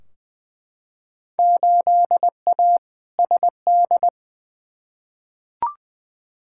Morse code, heard as a keyed tone, about 20 wpm, 700 Hz. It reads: 8A SD
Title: Field Day Exchanges Encore - 20wpm